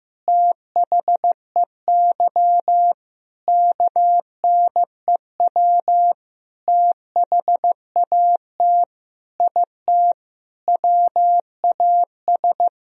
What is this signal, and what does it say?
Morse code, heard as a keyed tone, about 15 wpm, 700 Hz. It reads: THEY KNEW THAT IT WAS
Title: Sentences from Top 400 Words and Abbreviations 15wpm